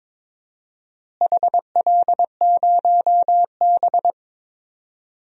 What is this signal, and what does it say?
Morse code, heard as a keyed tone, about 22 wpm, 700 Hz. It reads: HL0B
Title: DXCC-100 Call Signs - 4 Characters - Repeated 2x - 22wpm